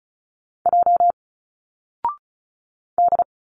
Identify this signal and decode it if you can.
Morse code, heard as a keyed tone, about 33 wpm, 700 Hz. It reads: J D
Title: Single Letters 35wpm